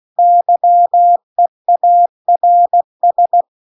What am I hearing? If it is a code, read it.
Morse code, heard as a keyed tone, about 16 wpm, 700 Hz. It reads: YEARS